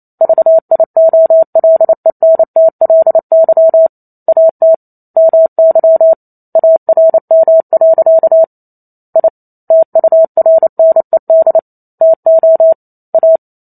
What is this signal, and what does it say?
Morse code, heard as a keyed tone, about 28 wpm, 650 Hz. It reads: VIOLENTLY AT MY ARM. I TURNED TO A